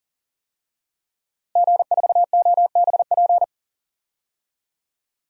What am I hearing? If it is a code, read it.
Morse code, heard as a keyed tone, about 40 wpm, 700 Hz. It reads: G4OBP
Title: DXCC-100 Call Signs - 5 Characters - Repeated 2x - 40wpm